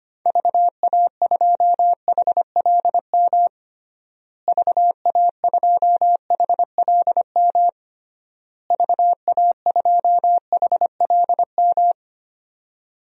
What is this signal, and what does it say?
Morse code, heard as a keyed tone, about 25 wpm, 700 Hz. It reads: VA2HLM VA2HLM VA2HLM
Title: DXCC-100 Call Signs - 6 Characters - Repeated 3x - 25wpm